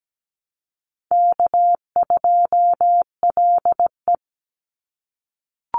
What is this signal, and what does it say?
Morse code, heard as a keyed tone, about 17 wpm, 700 Hz. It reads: K2LE
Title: Easy Call Signs - 17wpm